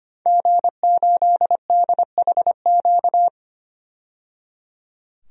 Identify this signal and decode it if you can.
Morse code, heard as a keyed tone, about 25 wpm, 700 Hz. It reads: G8DHQ
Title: DXCC-100 Call Signs - 5 Characters - 25wpm